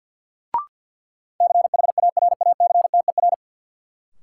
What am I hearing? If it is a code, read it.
Morse code, heard as a keyed tone, about 52 wpm, 700 Hz. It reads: CHARACTER
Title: Sentences from Top 1200 Words - New Word Drill - 50wpm